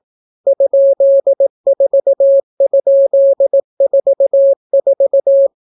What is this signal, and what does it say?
Morse code, heard as a keyed tone, about 18 wpm, 550 Hz. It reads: ?4?44